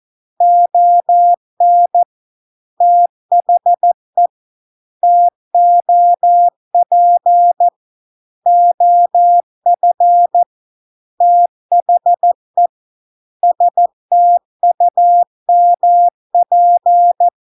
Morse code, heard as a keyed tone, about 14 wpm, 700 Hz. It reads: ON THE TOP OF THE STUMP